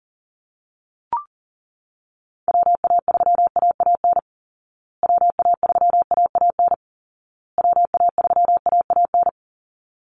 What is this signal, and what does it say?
Morse code, heard as a keyed tone, about 40 wpm, 700 Hz. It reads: WA3AAN WA3AAN WA3AAN
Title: Call Signs Repeated 2x - 40wpm